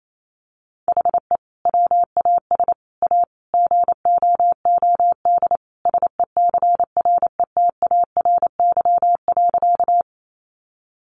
Morse code, heard as a keyed tone, about 28 wpm, 700 Hz. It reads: HE WAS A GOOD SECRETARY.
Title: Sentences from Top 1300 Words 28wpm